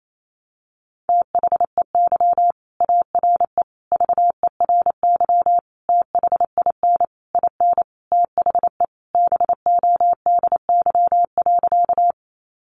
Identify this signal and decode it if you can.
Morse code, heard as a keyed tone, about 28 wpm, 700 Hz. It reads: THEY ARE VERY THIN IN THE BODY.
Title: Sentences from Top 1200 Words 28wpm